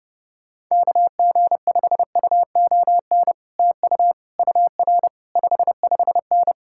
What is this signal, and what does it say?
Morse code, heard as a keyed tone, about 29 wpm, 700 Hz. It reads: KG5UON TU UR 55N